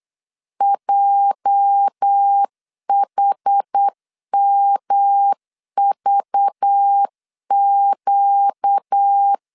Morse code, heard as a keyed tone, about 8 wpm, 800 Hz. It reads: JHMVQ
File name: LEZIONI / 9 28 8wpm 3512